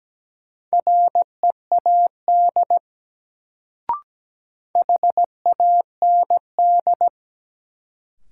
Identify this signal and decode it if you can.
Morse code, heard as a keyed tone, about 17 wpm, 700 Hz. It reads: READ HAND
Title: Sentences from Top 200 Words - New Word Drill - 17wpm